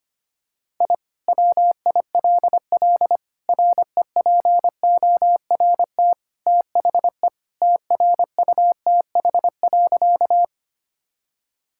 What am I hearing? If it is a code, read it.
Morse code, heard as a keyed tone, about 25 wpm, 700 Hz. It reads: I WILL REPORT THE TRUTH.
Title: Sentences from Top 1000 Words - Review Part 6 - 25wpm